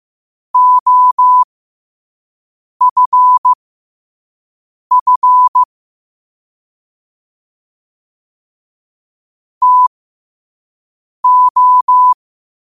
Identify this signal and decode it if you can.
Morse code, heard as a keyed tone, about 15 wpm, spaced as for 5 wpm, 1000 Hz. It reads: OFF TO
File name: qotdfiles / 5wpm